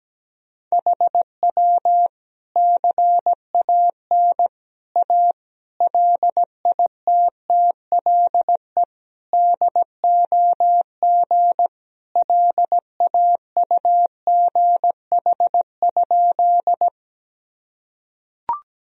Morse code, heard as a keyed tone, about 17 wpm, 700 Hz. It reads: HW CAN A LITTLE DOG LAUGH?
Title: Sentences from Top 500 Words and Abbreviations 17wpm